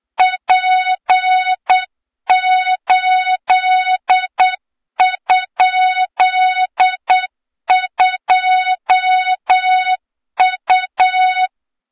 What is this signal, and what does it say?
Morse code, heard as a keyed tone, about 8 wpm, 3700 Hz, with noise. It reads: P8?2U
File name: Morze-HA2MN / hang / 23 GYAKORLAT